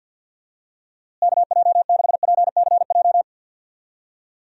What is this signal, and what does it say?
Morse code, heard as a keyed tone, about 54 wpm, 700 Hz. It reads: KJ6PCJ
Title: USA Call Signs - 6 Characters - Repeated 2x - 50wpm